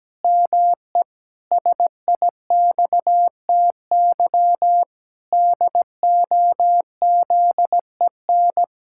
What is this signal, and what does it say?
Morse code, heard as a keyed tone, about 17 wpm, 700 Hz. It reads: ME SIXTY DOZEN
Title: Pangram Challenge v2 - 17wpm